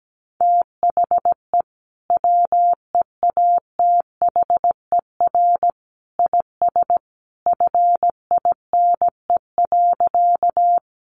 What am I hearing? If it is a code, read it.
Morse code, heard as a keyed tone, about 17 wpm, 700 Hz. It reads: THE WEATHER IS FINE.
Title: Sentences from Top 800 Words 17wpm